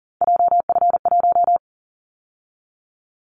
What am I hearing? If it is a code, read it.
Morse code, heard as a keyed tone, about 41 wpm, 700 Hz. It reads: JF1
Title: Callsign Prefixes for Top 25 DXCC - 40wpm